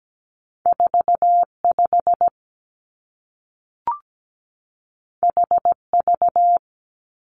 Morse code, heard as a keyed tone, about 17 wpm, 700 Hz. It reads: 4 5 H V
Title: Two Challenging Letter-Number Combinations 17wpm